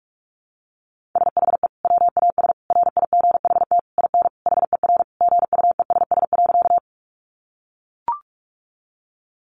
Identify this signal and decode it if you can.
Morse code, heard as a keyed tone, about 44 wpm, 700 Hz. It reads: SHE WAS RIGHT IN HER GUESS.
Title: Sentences from Top 1500 Words 45wpm